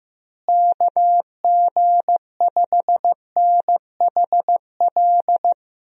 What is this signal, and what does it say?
Morse code, heard as a keyed tone, about 16 wpm, 700 Hz. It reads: KG5NHL
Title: USA Call Signs - 6 Characters - Repeated 2x - 15wpm